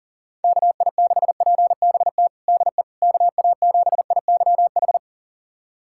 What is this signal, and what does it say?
Morse code, heard as a keyed tone, about 40 wpm, 700 Hz. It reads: KI6PBT DE KA7IYH
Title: USA Callsign DE Callsign - 6 Characters - 40wpm